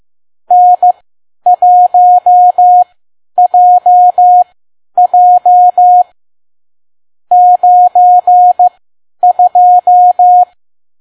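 Morse code, heard as a keyed tone, about 15 wpm, spaced as for 10 wpm, 720 Hz. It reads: N1JJ 92
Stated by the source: Morse code practice at 10 WPM.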